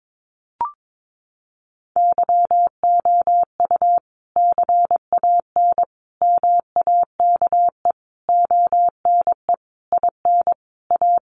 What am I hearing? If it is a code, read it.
Morse code, heard as a keyed tone, about 22 wpm, 700 Hz. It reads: YOU CAN MAKE ONE IN A
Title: Sentences from Top 1000 Words 22wpm